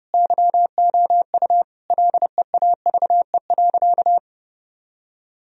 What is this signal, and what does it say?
Morse code, heard as a keyed tone, about 30 wpm, 700 Hz. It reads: YOU LEAVE.
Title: Sentences from Top 1000 Words - Review Part 4 - 30wpm